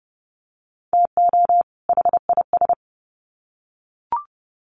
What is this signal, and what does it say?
Morse code, heard as a keyed tone, about 30 wpm, 700 Hz. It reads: TO HIS
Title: Sets of 2 Words 30wpm